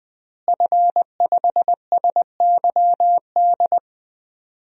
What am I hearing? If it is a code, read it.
Morse code, heard as a keyed tone, about 20 wpm, 700 Hz. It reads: F5SYD
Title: DXCC-100 Call Signs - 5 Characters - Repeated 2x - 20wpm